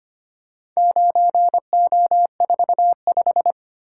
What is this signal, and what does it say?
Morse code, heard as a keyed tone, about 25 wpm, 700 Hz. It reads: 9O45
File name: practice / 25wpm / Four Letter-Number Combinations - 25wpm